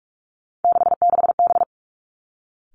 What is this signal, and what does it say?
Morse code, heard as a keyed tone, about 46 wpm, 700 Hz. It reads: 66B
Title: Three Challenging Letter-Number Combinations 45wpm